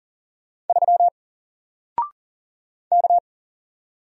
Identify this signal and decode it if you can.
Morse code, heard as a keyed tone, about 40 wpm, 700 Hz. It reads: 3 K
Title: Single Letter-Number Alt 40wpm